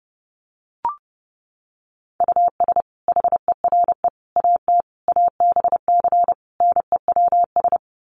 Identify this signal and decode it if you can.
Morse code, heard as a keyed tone, about 30 wpm, 700 Hz. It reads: US HERE AT ABC NEWS
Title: Sets of 5 Words 30wpm